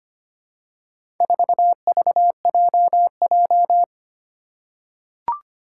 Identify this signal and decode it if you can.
Morse code, heard as a keyed tone, about 25 wpm, 700 Hz. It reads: 4VJJ
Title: Four Challenging Letter-Number Combinations 25wpm